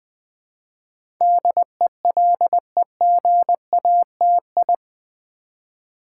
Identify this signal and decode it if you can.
Morse code, heard as a keyed tone, about 20 wpm, 700 Hz. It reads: DELEGATI
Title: Build A Word - 4 Letters At A Time - 10-Letter Words - 20wpm